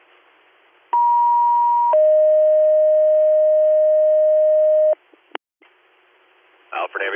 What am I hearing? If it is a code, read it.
Two-tone page, heard as a one-second tone, then a three-second tone, about 950 then 620 Hz, with speech. It is heard from 0.9 to 4.9 s.